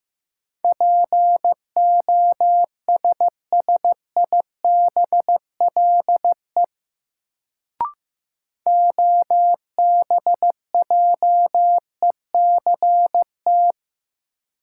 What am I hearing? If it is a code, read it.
Morse code, heard as a keyed tone, about 15 wpm, 700 Hz. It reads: POSSIBLE OBJECT
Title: Sentences from Top 500 Words - New Word Drill - 15wpm